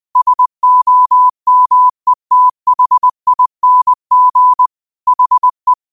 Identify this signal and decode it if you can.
Morse code, heard as a keyed tone, about 20 wpm, 1000 Hz. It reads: SOMETHING HE
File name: qotdfiles / 20wpm